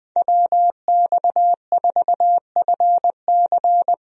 Morse code, heard as a keyed tone, about 20 wpm, 700 Hz. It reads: WX4FC
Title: USA Call Signs - 5 Characters - Repeated 3x - 20wpm